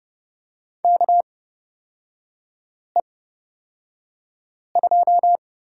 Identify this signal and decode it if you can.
Morse code, heard as a keyed tone, about 30 wpm, spaced as for 5 wpm, 700 Hz. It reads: KE2
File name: Lesson 37e - Letter x - Three Characters 30wpm